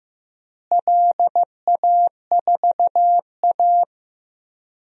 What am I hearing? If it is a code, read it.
Morse code, heard as a keyed tone, about 15 wpm, 700 Hz. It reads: LA4A
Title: DXCC-100 Call Signs - 4 Characters - Repeated 3x - 15wpm